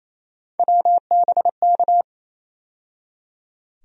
Morse code, heard as a keyed tone, about 28 wpm, 700 Hz. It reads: WBK
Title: USA Call Sign Postfixes - 3 Letters 28wpm